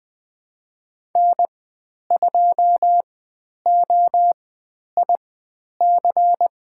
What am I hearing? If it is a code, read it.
Morse code, heard as a keyed tone, about 20 wpm, spaced as for 10 wpm, 700 Hz. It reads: N2OIC
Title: 102 Lesson 17g - Letter u - Call Signs 20wpm